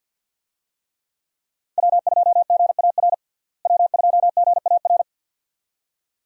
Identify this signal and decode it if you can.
Morse code, heard as a keyed tone, about 52 wpm, 700 Hz. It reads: W2GAR W2GAR
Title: USA Call Signs - 5 Characters - Repeated 2x - 50wpm